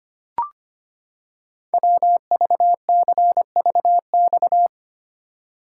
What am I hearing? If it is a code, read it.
Morse code, heard as a keyed tone, about 25 wpm, 700 Hz. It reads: WVCVX
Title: Code Groups - 5 Letter:Numbers - 25wpm